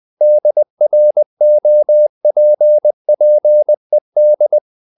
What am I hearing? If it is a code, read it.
Morse code, heard as a keyed tone, about 20 wpm, 600 Hz. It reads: DROPPED
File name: the old man and the sea / the old man and the sea part 1 20wpm